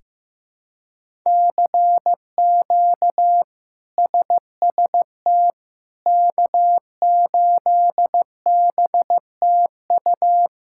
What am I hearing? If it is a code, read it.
Morse code, heard as a keyed tone, about 15 wpm, 700 Hz. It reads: CQ SST K8BTU
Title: K1USN SST QSO Practice - 15wpm